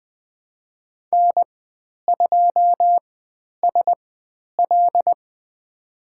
Morse code, heard as a keyed tone, about 20 wpm, spaced as for 10 wpm, 700 Hz. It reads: N2SL
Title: 095 Lesson 16g - Letter c - Call Signs 20wpm